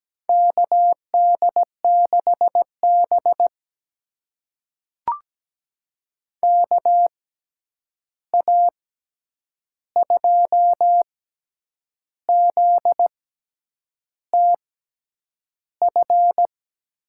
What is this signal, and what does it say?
Morse code, heard as a keyed tone, about 17 wpm, spaced as for 8 wpm, 700 Hz. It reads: KD6B KA2ZTF
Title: USA Call Signs - 6x Character Spacing - 17wpm